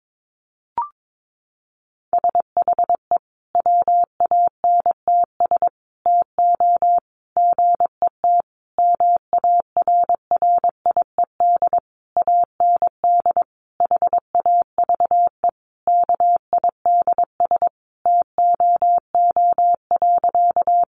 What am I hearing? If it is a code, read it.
Morse code, heard as a keyed tone, about 22 wpm, 700 Hz. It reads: SHE WANTS TO GET MARRIED AND HAVE KIDS TOO.
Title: Easy Phrases 22wpm